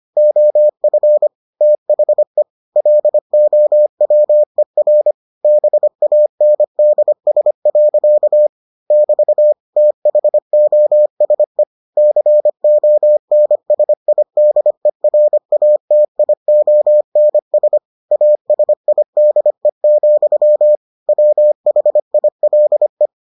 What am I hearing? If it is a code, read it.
Morse code, heard as a keyed tone, about 25 wpm, 600 Hz. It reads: OF THE LOWER BANDS. <BT> THOSE CONSIDERATIONS ASIDE, WHILE